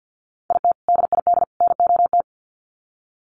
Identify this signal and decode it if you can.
Morse code, heard as a keyed tone, about 50 wpm, 700 Hz. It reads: IT DID NOT
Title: Sets of 3 Words 50wpm